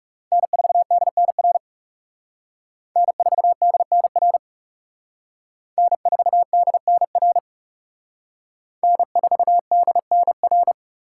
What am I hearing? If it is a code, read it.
Morse code, heard as a keyed tone, about 37 wpm, 700 Hz. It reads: N4DNR N4DNR N4DNR N4DNR
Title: Speed Racing - USA 1x3 Call Signs 30wpm